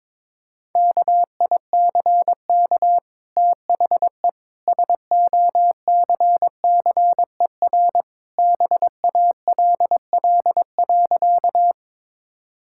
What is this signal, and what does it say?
Morse code, heard as a keyed tone, about 22 wpm, 690 Hz. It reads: KICK THE SOCCER BALL.
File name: practice / 22wpm / Easy Sentences with Hints - 22wpm